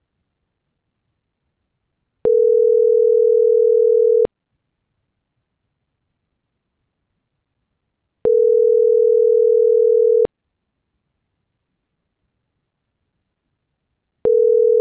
Ringback tone (North America), 440 + 480 Hz.